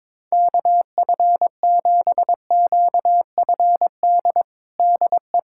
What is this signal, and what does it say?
Morse code, heard as a keyed tone, about 22 wpm, 700 Hz. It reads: KF7QFD DE
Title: USA Callsign DE Callsign - 6 Characters - 22wpm